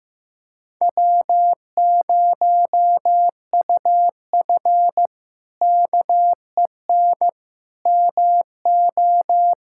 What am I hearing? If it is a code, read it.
Morse code, heard as a keyed tone, about 15 wpm, 700 Hz. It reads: W0UF KEN MO
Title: K1USN SST QSO Practice - 15wpm